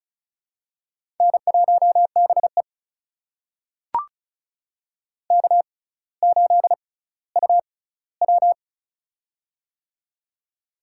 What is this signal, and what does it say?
Morse code, heard as a keyed tone, about 36 wpm, spaced as for 18 wpm, 700 Hz. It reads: N1BE K8UW
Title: USA Call Signs - 4 Characters - 6x Character Spacing - 35wpm